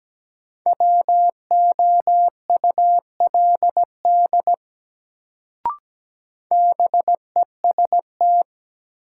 Morse code, heard as a keyed tone, about 17 wpm, 700 Hz. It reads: WOULD BEST
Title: Sets of 2 Words - New Word Drill - 17wpm